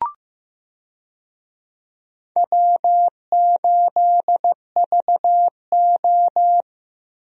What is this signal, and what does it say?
Morse code, heard as a keyed tone, about 15 wpm, 700 Hz. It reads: W8VO
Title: USA Call Signs - 4 Characters - Repeated 2x - 15wpm